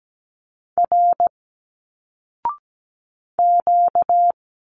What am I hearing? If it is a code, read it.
Morse code, heard as a keyed tone, about 17 wpm, 700 Hz. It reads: R Q
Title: Single Letters 17wpm